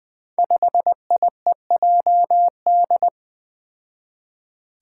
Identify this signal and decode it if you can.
Morse code, heard as a keyed tone, about 20 wpm, 700 Hz. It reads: 5IEJD
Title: Five Challenging Letter-Number Combinations 20wpm